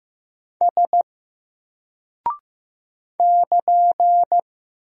Morse code, heard as a keyed tone, about 15 wpm, 700 Hz. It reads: S <KN>
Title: Single Character (Letter:Number:Punctuation:Prosign) 15wpm